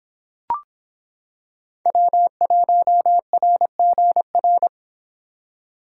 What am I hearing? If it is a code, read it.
Morse code, heard as a keyed tone, about 26 wpm, 700 Hz. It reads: W1RGR